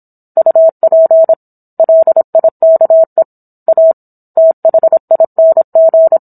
Morse code, heard as a keyed tone, about 27 wpm, 650 Hz. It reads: UP LIKE A THING